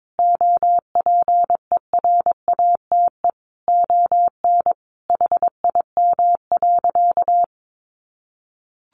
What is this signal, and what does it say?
Morse code, heard as a keyed tone, about 22 wpm, 700 Hz. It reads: OPERATE ON HIM.
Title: Sentences from Top 1500 Words 22wpm